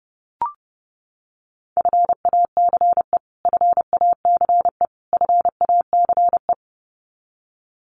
Morse code, heard as a keyed tone, about 30 wpm, 700 Hz. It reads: FACE FACE FACE
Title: Top 500 Words Repeated 30wpm